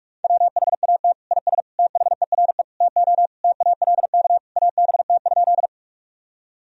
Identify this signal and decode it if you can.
Morse code, heard as a keyed tone, about 41 wpm, 700 Hz. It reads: WHAT IS THERE TO TALK ABT?
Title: Sentences from Top 400 Words and Abbreviations 45wpm